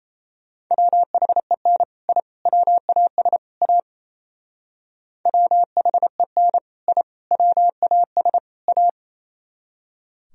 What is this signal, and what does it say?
Morse code, heard as a keyed tone, about 35 wpm, 700 Hz. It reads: WHEN I WAS A WHEN I WAS A